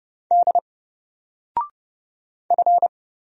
Morse code, heard as a keyed tone, about 29 wpm, 700 Hz. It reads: D F
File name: Single-Letter-Number-30wpm